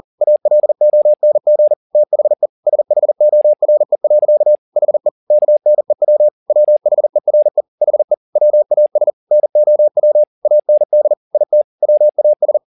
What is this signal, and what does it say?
Morse code, heard as a keyed tone, about 40 wpm, 600 Hz. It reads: ALONG THE SHORE. HE KNEW WHERE HE WAS NOW AND IT WAS
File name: the old man and the sea 2 40wpm